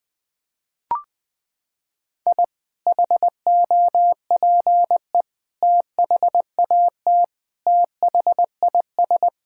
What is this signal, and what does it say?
Morse code, heard as a keyed tone, about 20 wpm, 700 Hz. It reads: I HOPE THAT THIS
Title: Sentences with 2-Word Phrases - 20wpm